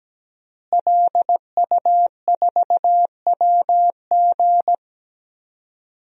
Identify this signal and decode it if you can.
Morse code, heard as a keyed tone, about 17 wpm, 700 Hz. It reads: LU4WG
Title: DXCC-100 Call Signs - 5 Characters - Repeated 3x - 17wpm